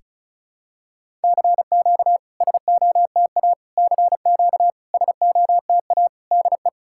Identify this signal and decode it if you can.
Morse code, heard as a keyed tone, about 36 wpm, 690 Hz. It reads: CQ SOTA CQ SOTA DE
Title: SOTA QSO Practice 35wpm